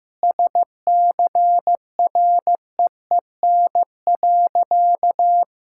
Morse code, heard as a keyed tone, about 15 wpm, 700 Hz. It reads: SCREEN.
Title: Sentences from Top 1000 Words - Review Part 6 - 15wpm